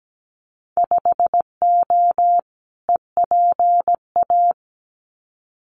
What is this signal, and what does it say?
Morse code, heard as a keyed tone, about 17 wpm, 700 Hz. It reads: 5O EPA
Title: Winter Field Day Exchanges Encore 17wpm